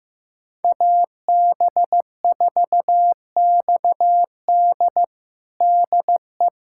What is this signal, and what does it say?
Morse code, heard as a keyed tone, about 15 wpm, 700 Hz. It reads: AB4XD DE
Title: USA Callsign DE Callsign - 5 Characters - 15wpm